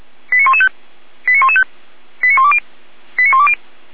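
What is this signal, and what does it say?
Five-tone selective call (ZVEI), heard as quick runs of five tones, tones of about 70 ms. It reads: 89115 89115 89120 89120
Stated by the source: ZVEI five-tone selective call received by SDR, from the feuerwehr-orlamuende.org fire brigade archive.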